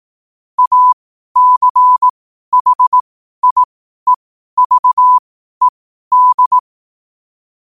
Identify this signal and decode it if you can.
Morse code, heard as a keyed tone, about 18 wpm, spaced as for 12 wpm, 1000 Hz. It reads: ACHIEVED
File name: qotdfiles / 12wpm